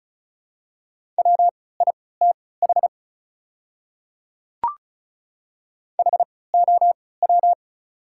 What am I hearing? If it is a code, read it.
Morse code, heard as a keyed tone, about 34 wpm, spaced as for 20 wpm, 690 Hz. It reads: WITH HOW
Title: 3x Character Spacing - Top 100 Words - 35wpm